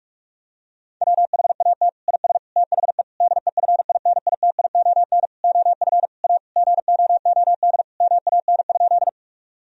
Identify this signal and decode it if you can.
Morse code, heard as a keyed tone, about 44 wpm, 700 Hz. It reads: WHAT IS THE DEFINITION OF A GOOD MAN?
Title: Sentences from Top 1000 Words - Review Part 4 - 45wpm